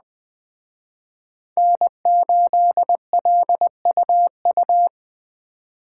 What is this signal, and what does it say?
Morse code, heard as a keyed tone, about 21 wpm, 700 Hz. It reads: N8LUU